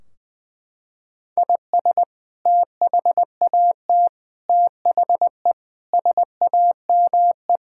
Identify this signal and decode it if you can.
Morse code, heard as a keyed tone, about 20 wpm, 700 Hz. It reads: IS THAT THE SAME
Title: Sentences from Top 1600 Words 20wpm